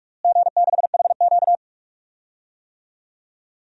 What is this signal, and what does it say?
Morse code, heard as a keyed tone, about 48 wpm, 700 Hz. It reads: G6HQ
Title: DXCC-100 Call Signs - 4 Characters - Repeated 3x - 45wpm